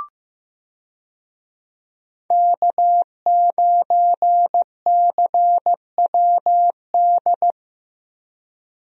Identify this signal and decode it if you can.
Morse code, heard as a keyed tone, about 15 wpm, 700 Hz. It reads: K9CWD
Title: USA Call Signs - 5 Characters - Repeated 2x - 15wpm